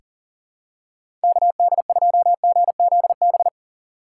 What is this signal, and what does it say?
Morse code, heard as a keyed tone, about 40 wpm, 700 Hz. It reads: KD2GZB
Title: USA Call Signs - 6 Characters - 2x Character Spacing - 40wpm